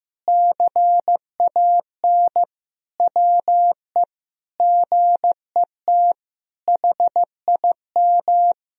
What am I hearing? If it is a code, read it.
Morse code, heard as a keyed tone, about 15 wpm, 700 Hz. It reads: CAN WE GET HIM
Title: Sentences from Top 1000 Words - Review Part 4 - 15wpm